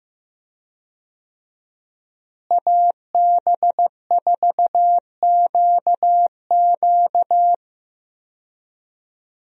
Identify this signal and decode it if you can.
Morse code, heard as a keyed tone, about 15 wpm, 700 Hz. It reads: AB4QQ
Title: USA Call Signs - 5 Characters - Repeated 2x - 15wpm